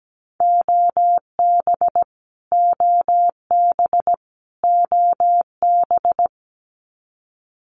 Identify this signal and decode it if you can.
Morse code, heard as a keyed tone, about 17 wpm, 700 Hz. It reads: OB OB OB
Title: Abbreviations Repeated 3x - 17wpm